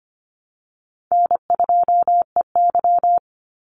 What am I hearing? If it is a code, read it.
Morse code, heard as a keyed tone, about 25 wpm, 700 Hz. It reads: N2EY
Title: Easy Call Signs Repeated 3x - 25wpm